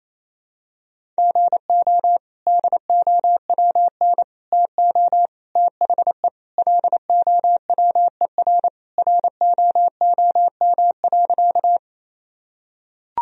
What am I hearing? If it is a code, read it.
Morse code, heard as a keyed tone, about 28 wpm, 700 Hz. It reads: GO DOWN TO THE LOWER ROOM.
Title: Sentences from Top 1000 Words - Review Part 4 - 28wpm